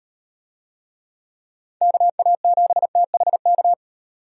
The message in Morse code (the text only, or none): KA7THK